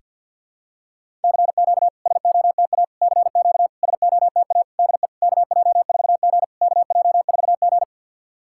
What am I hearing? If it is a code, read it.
Morse code, heard as a keyed tone, about 52 wpm, 690 Hz. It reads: CQ SOTA CQ SOTA DE KJ4G KJ4G
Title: SOTA QSO Practice 50wpm